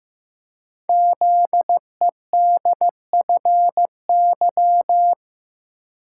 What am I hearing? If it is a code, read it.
Morse code, heard as a keyed tone, about 15 wpm, 700 Hz. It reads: ZEDFY
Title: Code Groups - 5 Letters - 15wpm